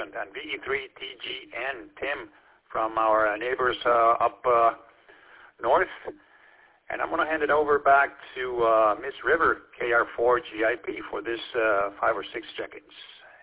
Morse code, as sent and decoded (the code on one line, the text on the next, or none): none